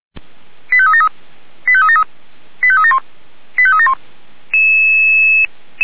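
Five-tone selective call (ZVEI), heard as quick runs of five tones, tones of about 70 ms. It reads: 85373 85373 85371 85371
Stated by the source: ZVEI five-tone call 89123 received by SDR, channel 464, recorded by feuerwehr-orlamuende fire brigade server.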